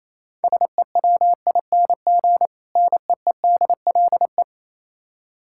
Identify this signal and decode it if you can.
Morse code, heard as a keyed tone, about 27 wpm, 700 Hz. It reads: SEWING NEEDLE
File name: Easy Words - Speed-Racing with Hints - 22wpm